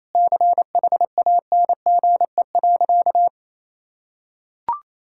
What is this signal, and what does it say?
Morse code, heard as a keyed tone, about 28 wpm, 700 Hz. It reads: CHANGE.
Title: Sentences from Top 1000 Words - Review Part 5 - 28wpm